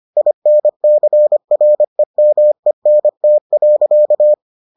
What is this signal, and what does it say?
Morse code, heard as a keyed tone, about 25 wpm, 600 Hz. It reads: INCREMENT.